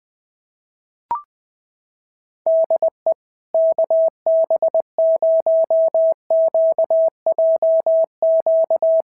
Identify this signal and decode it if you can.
Morse code, heard as a keyed tone, about 20 wpm, 650 Hz. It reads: DE KB0QJQ
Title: SOTA QSO Practice 20wpm